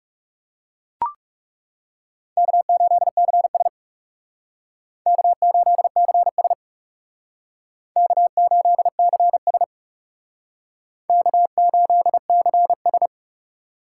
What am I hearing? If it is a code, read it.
Morse code, heard as a keyed tone, about 40 wpm, 700 Hz. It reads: K8CS K8CS K8CS K8CS